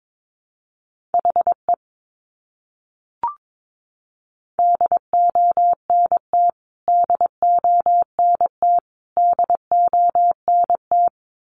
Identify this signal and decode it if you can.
Morse code, heard as a keyed tone, about 22 wpm, 700 Hz. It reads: HE DONT DONT DONT
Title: Top 500 Words Repeated 22wpm